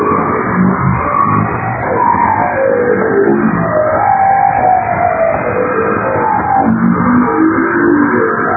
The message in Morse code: none

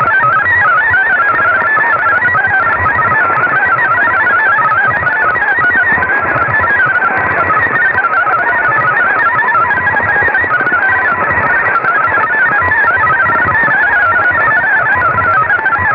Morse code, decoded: none

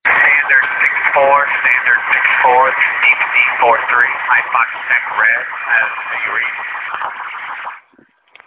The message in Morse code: none

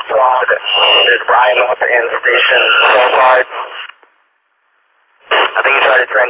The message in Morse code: none

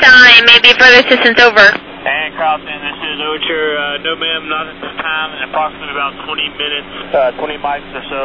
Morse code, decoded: none

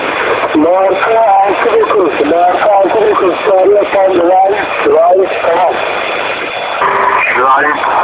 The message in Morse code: none

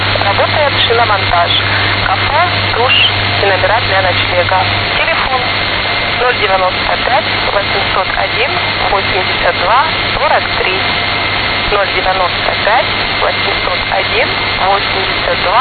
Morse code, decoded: none

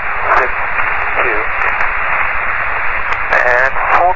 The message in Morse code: none

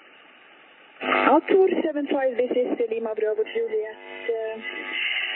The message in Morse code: none

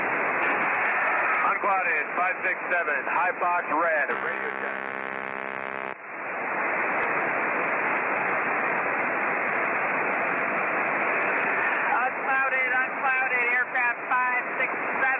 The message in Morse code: none

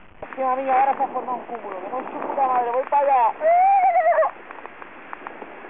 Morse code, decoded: none

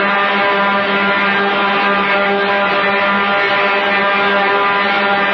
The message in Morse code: none

